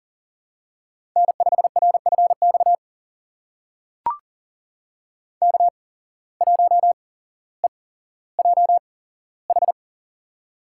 Morse code, heard as a keyed tone, about 42 wpm, spaced as for 17 wpm, 700 Hz. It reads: N5RFX K1EJH
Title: USA Call Signs - 5 Characters - 8x Character Spacing - 40wpm